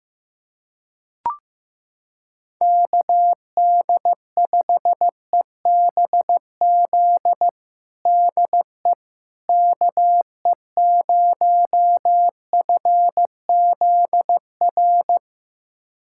Morse code, heard as a keyed tone, about 15 wpm, 700 Hz. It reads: KD5EBZ DE KE0FZR
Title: USA Callsign DE Callsign - 6 Characters - 15wpm